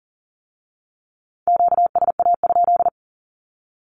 Morse code, heard as a keyed tone, about 40 wpm, 700 Hz. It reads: QSA?
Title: Q-Codes with Spoken Meaning - 40wpm